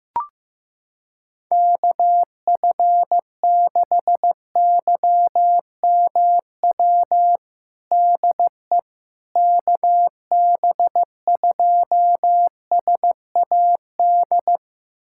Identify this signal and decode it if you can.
Morse code, heard as a keyed tone, about 15 wpm, 700 Hz. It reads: KF6YMW DE KB2SAD
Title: USA Callsign DE Callsign 15wpm